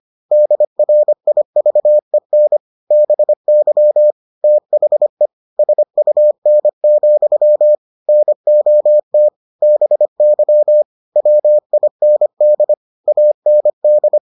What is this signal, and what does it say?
Morse code, heard as a keyed tone, about 25 wpm, 600 Hz. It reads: DRIVEN BY THE SUN, NOT BY WIND AND